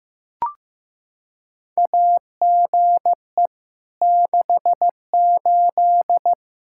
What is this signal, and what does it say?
Morse code, heard as a keyed tone, about 15 wpm, 700 Hz. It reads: AGE 68
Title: Practice QSO - Exchange 3 - More Info - 15wpm